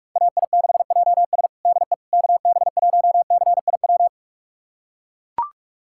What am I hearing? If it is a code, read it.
Morse code, heard as a keyed tone, about 41 wpm, 700 Hz. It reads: AI6JS DE KB1CIW
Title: USA Callsign DE Callsign 45wpm